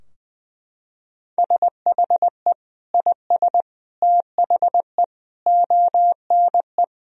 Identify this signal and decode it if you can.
Morse code, heard as a keyed tone, about 20 wpm, 700 Hz. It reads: SHE IS THE ONE
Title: Sentences from Top 1000 Words - Review Part 2 - 20wpm